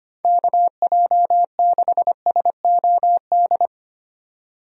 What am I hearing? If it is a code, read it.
Morse code, heard as a keyed tone, about 25 wpm, 700 Hz. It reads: KJ6SOD
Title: USA Call Signs - 6 Characters - 25wpm